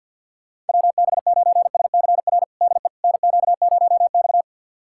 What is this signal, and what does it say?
Morse code, heard as a keyed tone, about 49 wpm, 700 Hz. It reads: WB9SCR DE NQ0X
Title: USA Callsign DE Callsign 50wpm